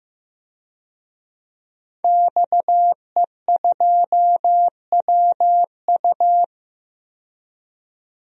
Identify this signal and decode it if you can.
Morse code, heard as a keyed tone, about 15 wpm, 700 Hz. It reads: XE2WU
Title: DXCC-100 Call Signs - 5 Characters - 15wpm